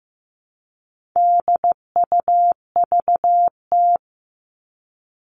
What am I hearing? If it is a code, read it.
Morse code, heard as a keyed tone, about 15 wpm, 700 Hz. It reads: DUVT
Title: All Cut Numbers - 4-Digit 15wpm